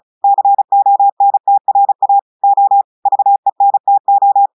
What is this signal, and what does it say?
Morse code, heard as a keyed tone, about 36 wpm, 800 Hz. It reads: CONTRA O VENTO
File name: audios / portugais / portugais texte en clair 1 vitesse 35 mots minutes